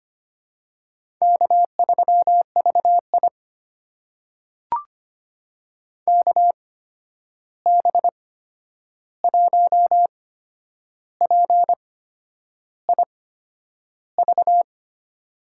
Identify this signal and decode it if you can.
Morse code, heard as a keyed tone, about 25 wpm, spaced as for 9 wpm, 700 Hz. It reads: K3VI KB1PIV
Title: USA Call Signs - 8x Character Spacing - 25wpm